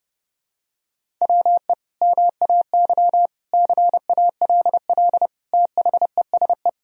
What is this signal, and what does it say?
Morse code, heard as a keyed tone, about 32 wpm, 700 Hz. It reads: WE MAY CALL THESE
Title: Sentences from Top 1000 Words - Review Part 1 - 30wpm